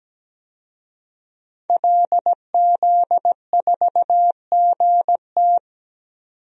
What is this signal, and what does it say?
Morse code, heard as a keyed tone, about 17 wpm, 700 Hz. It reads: LZ4GT
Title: DXCC-100 Call Signs - 5 Characters - 17wpm